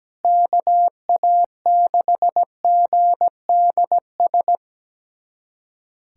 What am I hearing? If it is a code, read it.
Morse code, heard as a keyed tone, about 17 wpm, 700 Hz. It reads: KA6GDS
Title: USA Callsign DE Callsign - 6 Characters - 17wpm